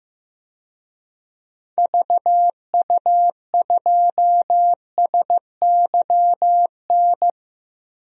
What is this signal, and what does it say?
Morse code, heard as a keyed tone, about 15 wpm, 700 Hz. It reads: VU2SYN
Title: DXCC-100 Call Signs - 6 Characters - 15wpm